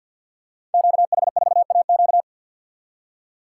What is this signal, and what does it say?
Morse code, heard as a keyed tone, about 52 wpm, 700 Hz. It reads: QH4AQ
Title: Code Groups - 5 Letter:Numbers - 50wpm